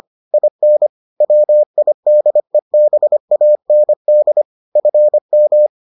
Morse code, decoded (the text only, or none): IN WIDEBAND FM